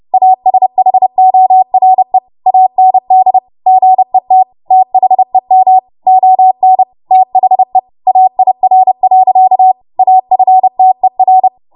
Morse code, heard as a keyed tone, about 30 wpm, 750 Hz. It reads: ASHORE AND GET THEM ON THE AIR. AFTER
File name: Morse / 30WPM